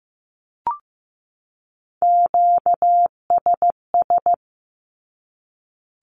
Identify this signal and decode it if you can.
Morse code, heard as a keyed tone, about 15 wpm, 700 Hz. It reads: QSS
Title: Q-Codes 15wpm